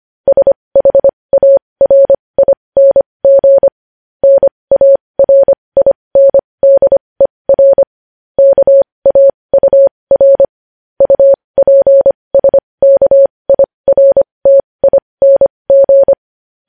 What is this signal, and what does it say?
Morse code, heard as a keyed tone, about 25 wpm, spaced as for 20 wpm, 550 Hz. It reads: SHARING NARINDER KAUR UPSKIRTING
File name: podcast / News headlines-1x-25x20